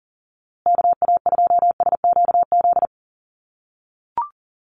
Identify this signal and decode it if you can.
Morse code, heard as a keyed tone, about 40 wpm, 700 Hz. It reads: KA2SQZ
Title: USA Call Signs Repeated 2x 40wpm